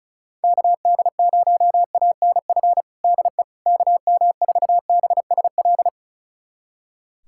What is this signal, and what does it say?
Morse code, heard as a keyed tone, about 34 wpm, 690 Hz. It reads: KD0ANF DE KM4BSL
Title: USA Callsign DE Callsign - 6 Characters - 35wpm